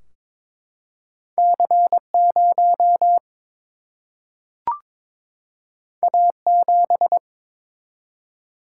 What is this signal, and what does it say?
Morse code, heard as a keyed tone, about 22 wpm, 700 Hz. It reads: C 0 A 7
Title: Single-Letter Single-Number 22wpm